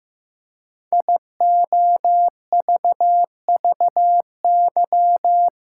Morse code, heard as a keyed tone, about 15 wpm, 700 Hz. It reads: IOVVY
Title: Code Groups - 5 Letter:Numbers - 15wpm